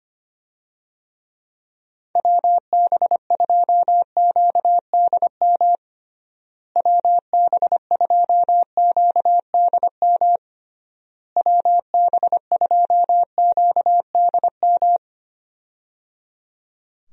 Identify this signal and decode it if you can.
Morse code, heard as a keyed tone, about 25 wpm, 700 Hz. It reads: WB2QDM WB2QDM WB2QDM
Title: USA Call Signs - 6 Characters - Repeated 3x - 25wpm